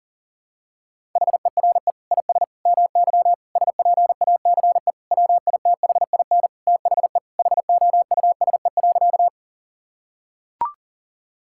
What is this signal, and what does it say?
Morse code, heard as a keyed tone, about 40 wpm, 700 Hz. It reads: HERE IS MY SPACE WITHIN THE HOUSE.
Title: Sentences from Top 1000 Words - Review Part 5 - 40wpm